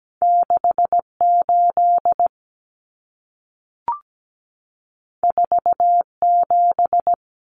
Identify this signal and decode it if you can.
Morse code, heard as a keyed tone, about 17 wpm, 700 Hz. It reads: 68 47
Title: Numbers - 2-Digit 17wpm